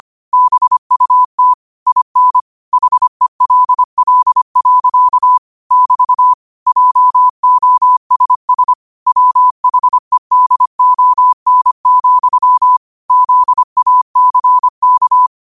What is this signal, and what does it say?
Morse code, heard as a keyed tone, about 25 wpm, 1000 Hz. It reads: BUT IN HELL. <BT> JOSS WHEDON, ZACK
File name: qotdfiles / 25wpm